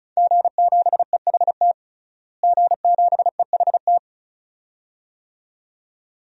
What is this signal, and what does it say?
Morse code, heard as a keyed tone, about 36 wpm, 700 Hz. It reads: G7EHT G7EHT
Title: DXCC-100 Call Signs - 5 Characters - Repeated 2x - 35wpm